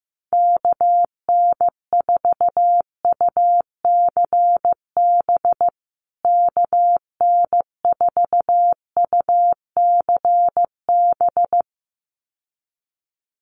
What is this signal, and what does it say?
Morse code, heard as a keyed tone, about 15 wpm, 700 Hz. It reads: KN4UCB KN4UCB
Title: USA Call Signs Repeated 2x 15wpm